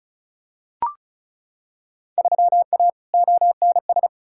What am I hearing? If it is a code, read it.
Morse code, heard as a keyed tone, about 36 wpm, 700 Hz. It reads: 3A ONS
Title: Field Day Exchanges - 35wpm